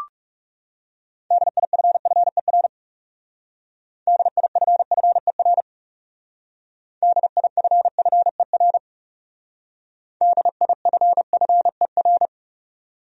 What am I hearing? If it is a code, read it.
Morse code, heard as a keyed tone, about 38 wpm, 700 Hz. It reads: DIFFER DIFFER DIFFER DIFFER